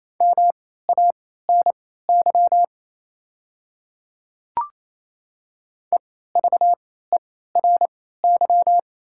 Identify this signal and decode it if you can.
Morse code, heard as a keyed tone, about 28 wpm, spaced as for 16 wpm, 700 Hz. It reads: MANY EVERY
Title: 3x Character Spacing - Top 300 Words - 28wpm